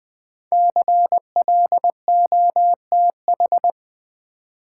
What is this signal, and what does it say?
Morse code, heard as a keyed tone, about 20 wpm, 700 Hz. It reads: CLOTH
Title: Build A Word - 1 Letter At A Time - 5-Letter Words - 20wpm